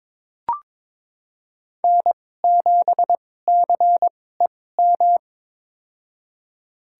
Morse code, heard as a keyed tone, about 22 wpm, spaced as for 16 wpm, 700 Hz. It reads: N7CEM
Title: USA Call Signs - 5 Characters - 2x Character Spacing - 22wpm